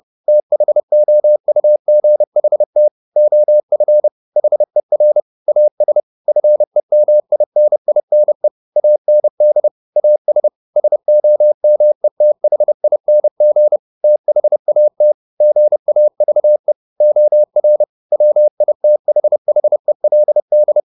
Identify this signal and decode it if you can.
Morse code, heard as a keyed tone, about 30 wpm, 600 Hz. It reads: THOUGHT OF HER AS FEMININE AND AS SOMETHING THAT GAVE OR WITHHELD
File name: the old man and the sea / the old man and the sea 1 30wpm